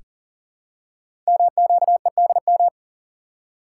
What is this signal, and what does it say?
Morse code, heard as a keyed tone, about 42 wpm, 700 Hz. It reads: MQEDM
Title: Code Groups - 5 Letter:Numbers - 40wpm